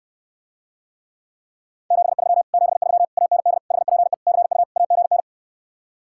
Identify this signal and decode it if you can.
Morse code, heard as a keyed tone, about 69 wpm, 700 Hz. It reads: 73 73 AND HPE CU AGN